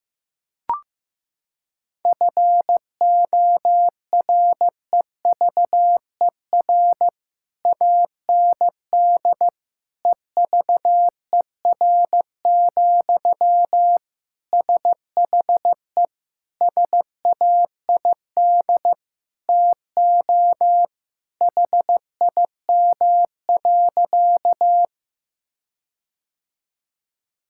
Morse code, heard as a keyed tone, about 15 wpm, 700 Hz. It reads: FOREVER AND EVER, SHE SAID TO HIM.
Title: Sentences from Top 1600 Words 15wpm